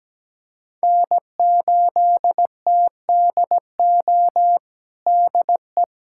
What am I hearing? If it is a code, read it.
Morse code, heard as a keyed tone, about 17 wpm, 700 Hz. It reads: N8TDO DE
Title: USA Callsign DE Callsign - 5 Characters - 17wpm